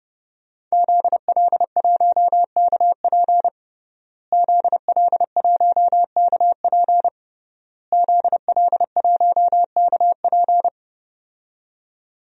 Morse code, heard as a keyed tone, about 30 wpm, 700 Hz. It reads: ZL1KP ZL1KP ZL1KP
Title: DXCC-100 Call Signs - 5 Characters - Repeated 3x - 30wpm